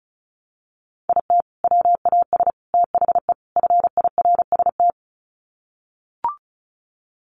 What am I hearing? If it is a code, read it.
Morse code, heard as a keyed tone, about 34 wpm, 700 Hz. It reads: IT WAS THE FIRST
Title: Sets of 4 Words 35wpm